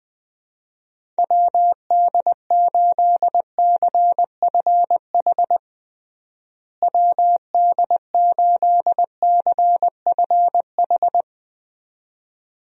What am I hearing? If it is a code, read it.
Morse code, heard as a keyed tone, about 20 wpm, 700 Hz. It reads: WD8CFH WD8CFH
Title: USA Call Signs - 6 Characters - Repeated 2x - 20wpm